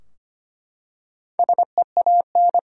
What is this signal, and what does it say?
Morse code, heard as a keyed tone, about 25 wpm, 700 Hz. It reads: SEAN
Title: USA Common Names - Max 5 Letters - 25wpm